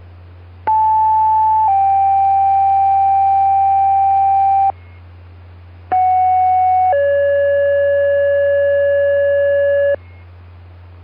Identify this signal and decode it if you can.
Two-tone page, heard as a one-second tone, then a three-second tone, about 870 then 780 Hz, about 740 then 570 Hz. It is heard from 0.7 to 10.0 s.